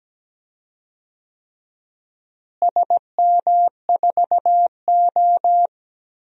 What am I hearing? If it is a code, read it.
Morse code, heard as a keyed tone, about 17 wpm, 700 Hz. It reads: SM4O
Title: DXCC-100 Call Signs - 4 Characters - Repeated 3x - 17wpm